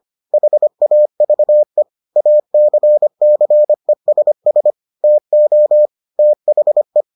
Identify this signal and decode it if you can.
Morse code, heard as a keyed tone, about 25 wpm, 600 Hz. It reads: HAVE ACCESS TO THE